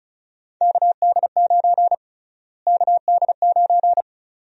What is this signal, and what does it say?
Morse code, heard as a keyed tone, about 34 wpm, 700 Hz. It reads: KD9 KD9
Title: USA Call Sign Prefixes - Repeated 2x - 35wpm